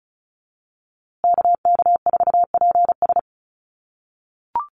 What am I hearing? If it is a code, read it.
Morse code, heard as a keyed tone, about 34 wpm, 700 Hz. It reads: KK4PS
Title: USA Call Signs Repeated 3x 35wpm